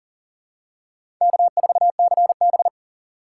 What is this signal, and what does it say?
Morse code, heard as a keyed tone, about 40 wpm, 700 Hz. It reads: K4CB
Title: USA Call Signs - 4 Characters - 2x Character Spacing - 40wpm